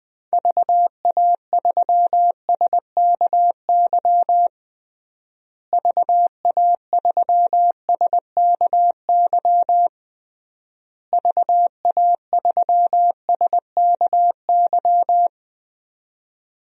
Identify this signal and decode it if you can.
Morse code, heard as a keyed tone, about 20 wpm, 700 Hz. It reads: VA3SKY VA3SKY VA3SKY
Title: DXCC-100 Call Signs - 6 Characters - Repeated 3x - 20wpm